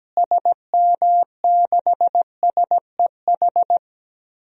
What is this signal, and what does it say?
Morse code, heard as a keyed tone, about 17 wpm, 700 Hz. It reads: SM6SEH
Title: DXCC-100 Call Signs - 6 Characters - Repeated 2x - 17wpm